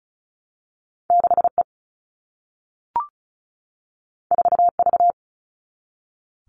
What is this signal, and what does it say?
Morse code, heard as a keyed tone, about 36 wpm, 700 Hz. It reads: 6E 4V
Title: All Cut Numbers - 2-Digit 35wpm